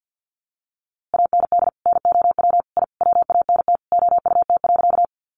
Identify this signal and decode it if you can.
Morse code, heard as a keyed tone, about 50 wpm, 700 Hz. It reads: AND NOW I WANT OUT.
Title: A Sound Of Thunder 50wpm